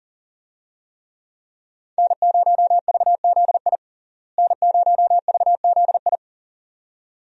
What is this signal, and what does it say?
Morse code, heard as a keyed tone, about 37 wpm, 700 Hz. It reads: N0VZI N0VZI
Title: USA Call Signs - 5 Characters - Repeated 2x - 40wpm